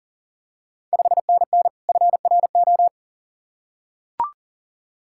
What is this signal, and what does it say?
Morse code, heard as a keyed tone, about 40 wpm, 690 Hz. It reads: 5NN FRO
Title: Ontario QSO Party Exchange 40wpm